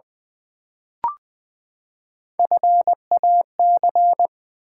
Morse code, heard as a keyed tone, about 20 wpm, 700 Hz. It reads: FAC
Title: Build A Word - 3 Letters At A Time - 8-Letter Words - 20wpm